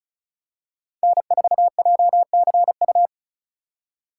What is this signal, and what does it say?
Morse code, heard as a keyed tone, about 36 wpm, 700 Hz. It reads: N4JCU
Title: USA Call Signs - 5 Characters - 2x Character Spacing - 35wpm